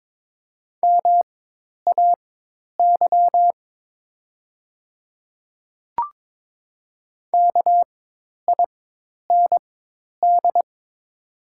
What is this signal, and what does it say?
Morse code, heard as a keyed tone, about 22 wpm, spaced as for 10 wpm, 700 Hz. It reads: MAY KIND
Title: 4x Character Spacing - Top 200 Words - 22wpm